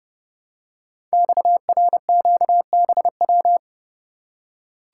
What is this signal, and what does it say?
Morse code, heard as a keyed tone, about 30 wpm, 700 Hz. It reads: XRQBW
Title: Code Groups - 5 Letters - 30wpm